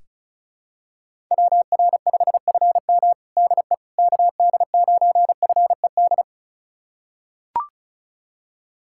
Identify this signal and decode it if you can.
Morse code, heard as a keyed tone, about 36 wpm, 700 Hz. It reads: WR5FM DE KD9FED
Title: USA Callsign DE Callsign 35wpm